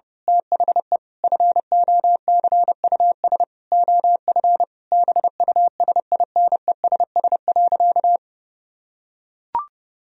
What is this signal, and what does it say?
Morse code, heard as a keyed tone, about 30 wpm, 700 Hz. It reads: THE FOCUS OF BUSINESS.
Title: Sentences from Top 1000 Words - Review Part 4 - 30wpm